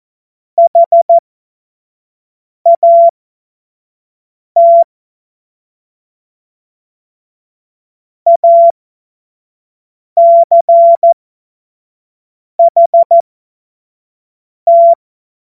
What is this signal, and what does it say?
Morse code, heard as a keyed tone, about 14 wpm, spaced as for 5 wpm, 680 Hz. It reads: HAT ACHT
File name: Morsekurs / Witze / Witz 05. April 2022 25 F70